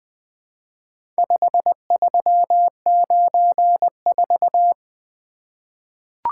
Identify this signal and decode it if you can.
Morse code, heard as a keyed tone, about 20 wpm, 700 Hz. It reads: 5394
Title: Four Letter-Number Combinations - 20wpm